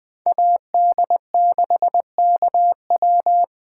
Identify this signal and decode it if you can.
Morse code, heard as a keyed tone, about 20 wpm, 700 Hz. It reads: AD6KW
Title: Speed Racing - USA 2x2 Call Signs 17wpm